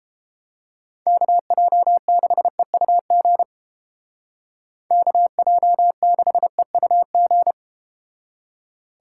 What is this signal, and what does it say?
Morse code, heard as a keyed tone, about 30 wpm, 700 Hz. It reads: KJ6EUG KJ6EUG